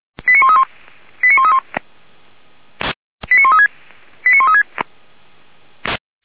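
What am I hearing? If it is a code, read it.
Five-tone selective call (ZVEI), heard as quick runs of five tones, tones of about 80 ms. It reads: 89131 89131 89136 89136
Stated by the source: ZVEI five-tone selective call, code 89138, received by SDR, from Feuerwehr Orlamünde.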